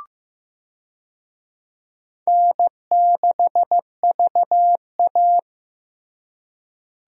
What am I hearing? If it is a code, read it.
Morse code, heard as a keyed tone, about 15 wpm, 700 Hz. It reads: N6VA
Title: USA Call Signs - 4 Characters - Repeated 3x - 15wpm